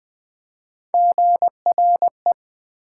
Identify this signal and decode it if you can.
Morse code, heard as a keyed tone, about 20 wpm, 700 Hz. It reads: GRE
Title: Build A Word - 1 Letter At A Time - 3 to 5-Letter Words - 20wpm